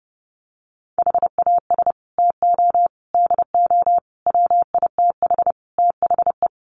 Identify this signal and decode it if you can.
Morse code, heard as a keyed tone, about 30 wpm, 700 Hz. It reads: HAS TO DO WITH THE
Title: Sets of 5 Words 30wpm